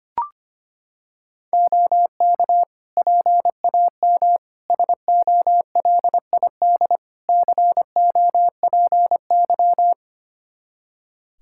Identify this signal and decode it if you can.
Morse code, heard as a keyed tone, about 25 wpm, 700 Hz. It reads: OK PAM SOLID COPY
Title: K1USN SST QSO Practice 25wpm